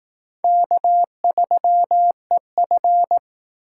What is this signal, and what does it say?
Morse code, heard as a keyed tone, about 18 wpm, 700 Hz. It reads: K3EF
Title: Speed Racing - USA 1x2 Call Signs 15wpm